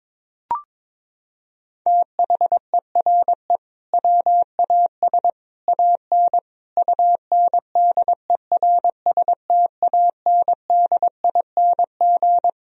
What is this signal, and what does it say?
Morse code, heard as a keyed tone, about 21 wpm, 700 Hz. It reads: THERE WAS AN UNDERSTANDING
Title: Sentences from Top 1000 Words - Review Part 3 - 22wpm